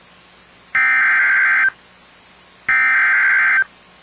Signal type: Emergency Alert System (SAME) tones